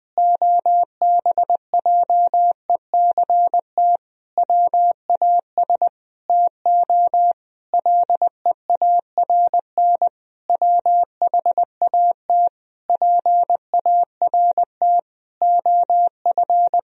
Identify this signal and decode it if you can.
Morse code, heard as a keyed tone, about 20 wpm, 700 Hz. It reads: OBJECT WAS TO LEARN WHAT PART OF
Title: Sentences from Top 1000 Words - Review Part 3 - 20wpm